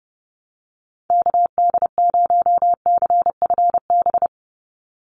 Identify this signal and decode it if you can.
Morse code, heard as a keyed tone, about 30 wpm, 700 Hz. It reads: KD0CFB
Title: USA Call Signs 30wpm